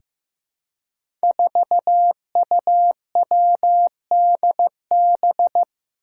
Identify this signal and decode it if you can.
Morse code, heard as a keyed tone, about 15 wpm, 700 Hz. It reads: 4UWDB
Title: Five Challenging Letter-Number Combinations 15wpm